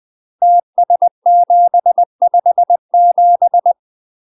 Morse code, heard as a keyed tone, about 20 wpm, 700 Hz. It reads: TS757
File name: UK MASTER QSOs TRAINING / 20WPM / UK MASTER QSOs 0001